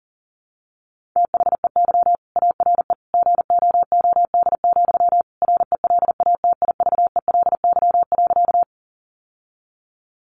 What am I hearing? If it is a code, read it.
Morse code, heard as a keyed tone, about 40 wpm, 700 Hz. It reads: THEY ARE GOOD, RELATIVELY.
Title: Sentences from Top 1100 Words 40wpm